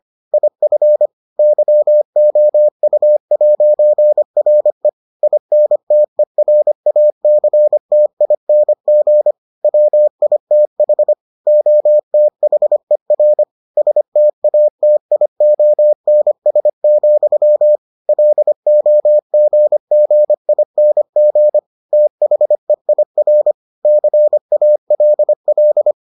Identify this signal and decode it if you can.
Morse code, heard as a keyed tone, about 25 wpm, 600 Hz. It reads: IF YOU'RE INTERACTING WITH OTHER STATIONS, LOGGING THEIR CALL